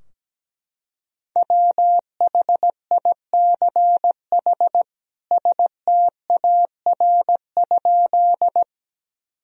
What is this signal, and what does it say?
Morse code, heard as a keyed tone, about 17 wpm, 700 Hz. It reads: WHICH STAR?
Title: Sentences from Top 1000 Words - Review Part 2 - 17wpm